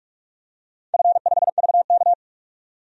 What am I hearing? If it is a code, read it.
Morse code, heard as a keyed tone, about 42 wpm, 700 Hz. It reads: F5VK
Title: DXCC-100 Call Signs - 4 Characters - 45wpm